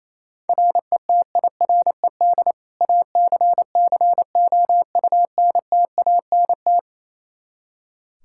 Morse code, heard as a keyed tone, about 28 wpm, 700 Hz. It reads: RETIRED ACCOUNTANT
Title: Practice QSO - Exchange 3 - More Info - 28wpm